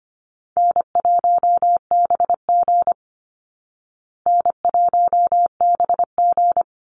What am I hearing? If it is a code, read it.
Morse code, heard as a keyed tone, about 25 wpm, 700 Hz. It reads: N1BG N1BG
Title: Easy Call Signs Repeated 2x - 25wpm